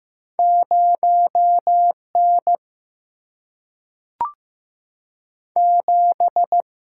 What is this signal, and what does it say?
Morse code, heard as a keyed tone, about 15 wpm, 700 Hz. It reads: 0N 7
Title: Single-Letter Single-Number 15wpm